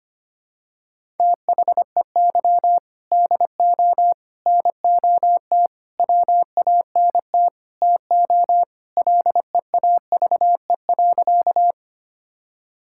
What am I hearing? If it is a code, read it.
Morse code, heard as a keyed tone, about 25 wpm, 700 Hz. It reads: THEY DO NOT WANT TO LEAVE.
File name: practice / 25wpm / Sentences with 3-Word Phrases - 25wpm